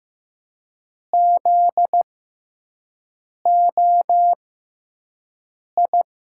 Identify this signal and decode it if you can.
Morse code, heard as a keyed tone, about 15 wpm, spaced as for 5 wpm, 700 Hz. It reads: ZOI
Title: USA Call Signs - 6 Characters - 6x Character Spacing - 15wpm